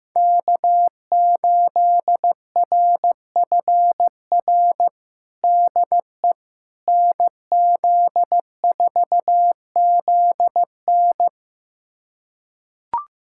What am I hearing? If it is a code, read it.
Morse code, heard as a keyed tone, about 15 wpm, 700 Hz. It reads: K8RFR DE NZ4ZN
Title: USA Callsign DE Callsign - 5 Characters - 15wpm